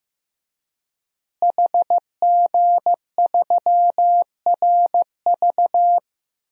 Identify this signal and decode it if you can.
Morse code, heard as a keyed tone, about 15 wpm, 700 Hz. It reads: HG3RV
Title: DXCC-100 Call Signs - 5 Characters - Repeated 2x - 15wpm